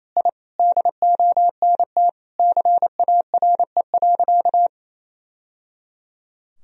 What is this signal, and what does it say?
Morse code, heard as a keyed tone, about 28 wpm, 700 Hz. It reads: I DONT CARE.
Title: Sentences from Top 1000 Words - Review Part 1 - 28wpm